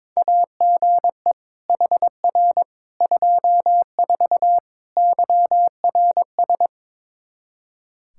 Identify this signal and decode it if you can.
Morse code, heard as a keyed tone, about 22 wpm, 700 Hz. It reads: AGE HR 24 YRS
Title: Practice QSO - Exchange 3 - More Info - 22wpm